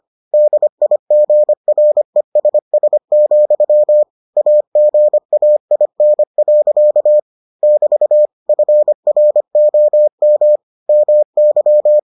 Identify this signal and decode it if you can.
Morse code, heard as a keyed tone, about 25 wpm, 600 Hz. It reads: DIGRESS, AGAIN. <BT> FROM MY